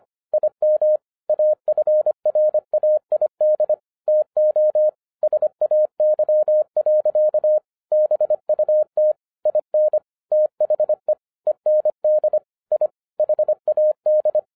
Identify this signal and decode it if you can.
Morse code, heard as a keyed tone, about 24 wpm, 610 Hz. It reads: IM AFRAID TO SAY. BUT IN THE END I HAD
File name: FIRST FISTS CONTACT-25wpm